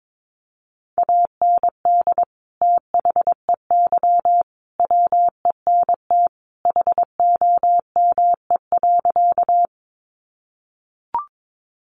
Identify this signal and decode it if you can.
Morse code, heard as a keyed tone, about 22 wpm, 700 Hz. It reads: AND THEY WENT HOME.
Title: Sentences from Top 200 Words 22wpm